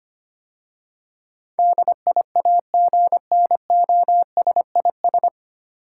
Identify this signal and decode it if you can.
Morse code, heard as a keyed tone, about 25 wpm, 700 Hz. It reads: DIAGNOSIS
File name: practice / 25wpm / 9 Letter Words - First 3 Letters Given - 25wpm